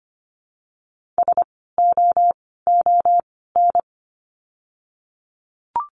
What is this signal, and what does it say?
Morse code, heard as a keyed tone, about 25 wpm, spaced as for 16 wpm, 700 Hz. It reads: SOON
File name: course / 25:10wpm / 094 Lesson 16f - Letter c - Words 25wpm